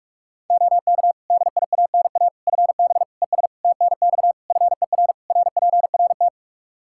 Morse code, heard as a keyed tone, about 41 wpm, 700 Hz. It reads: OK DIANA FB ES TNX FER RPRT
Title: Practice QSO - Exchange 2 - Info - 45wpm